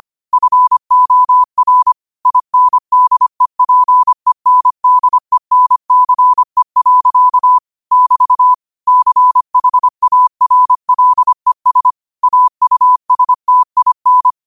Morse code, heard as a keyed tone, about 25 wpm, 1000 Hz. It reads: FOR INDEPENDENCE. <BT> CHARLES AUSTIN
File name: qotdfiles / 25wpm